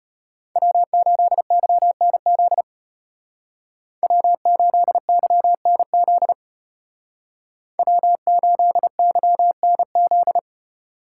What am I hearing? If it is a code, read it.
Morse code, heard as a keyed tone, about 33 wpm, 700 Hz. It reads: W8YNZ W8YNZ W8YNZ